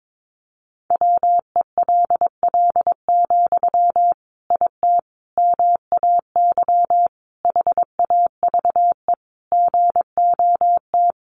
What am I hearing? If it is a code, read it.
Morse code, heard as a keyed tone, about 22 wpm, 700 Hz. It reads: WELL, IT MAY HAVE GOT
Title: Sentences from Top 1500 Words 22wpm